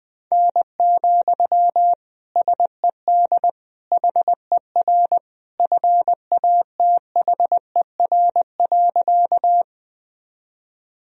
Morse code, heard as a keyed tone, about 20 wpm, 700 Hz. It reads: N, SED HER FATHER.
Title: Sentences from Top 300 Words and Abbreviations 20wpm